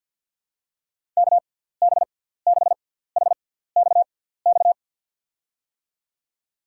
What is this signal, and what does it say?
Morse code, heard as a keyed tone, about 47 wpm, 700 Hz. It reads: K B 6 H X X
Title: USA Call Signs - 6 Characters - 6x Character Spacing - 50wpm